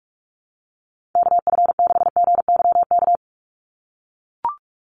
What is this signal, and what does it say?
Morse code, heard as a keyed tone, about 44 wpm, 700 Hz. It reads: KF6GYK
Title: USA Call Signs 45wpm